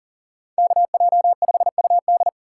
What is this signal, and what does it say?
Morse code, heard as a keyed tone, about 50 wpm, 700 Hz. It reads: KJ5UD
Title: Speed Racing - USA 2x2 Call Signs 40wpm